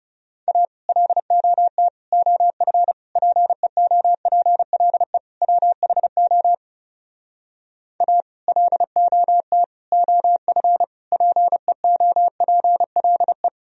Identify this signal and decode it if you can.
Morse code, heard as a keyed tone, about 40 wpm, 700 Hz. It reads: A LOT OF PEOPLE WHO A LOT OF PEOPLE